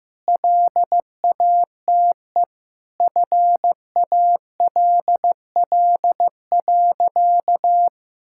Morse code, heard as a keyed tone, about 15 wpm, 700 Hz. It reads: LATE FALL.
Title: Sentences from Top 1000 Words - Review Part 6 - 15wpm